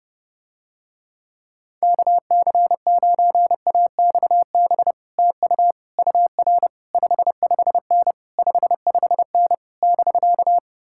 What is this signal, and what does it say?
Morse code, heard as a keyed tone, about 30 wpm, 700 Hz. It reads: KC9AXB TU UR 55N 55N <BK>